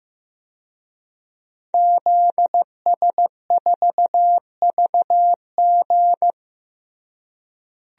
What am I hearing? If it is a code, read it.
Morse code, heard as a keyed tone, about 15 wpm, 700 Hz. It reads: ZS4VG
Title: DXCC-100 Call Signs - 5 Characters - Repeated 3x - 15wpm